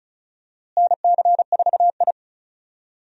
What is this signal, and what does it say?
Morse code, heard as a keyed tone, about 36 wpm, 700 Hz. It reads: NC4I
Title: USA Call Signs - 4 Characters - Repeated 2x - 35wpm